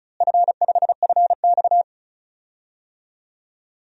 Morse code, heard as a keyed tone, about 36 wpm, 700 Hz. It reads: F5FX
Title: DXCC-100 Call Signs - 4 Characters - Repeated 3x - 35wpm